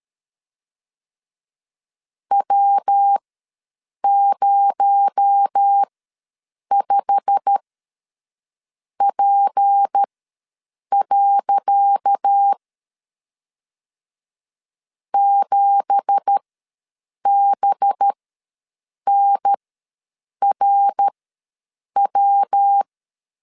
Morse code, heard as a keyed tone, about 13 wpm, spaced as for 7 wpm, 800 Hz. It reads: W05P. 7BNRW
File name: LEZIONI / 10 SIGNS / 10C DOT ws 12wpm 3548